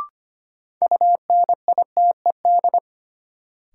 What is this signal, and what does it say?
Morse code, heard as a keyed tone, about 25 wpm, 700 Hz. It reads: UNITED
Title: Sets of 2 Words - New Word Drill - 25wpm